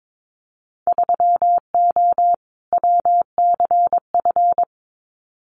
Morse code, heard as a keyed tone, about 22 wpm, 700 Hz. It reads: 3O WCF
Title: Winter Field Day Exchanges 22wpm